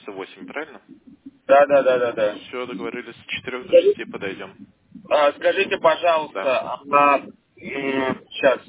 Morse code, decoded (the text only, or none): none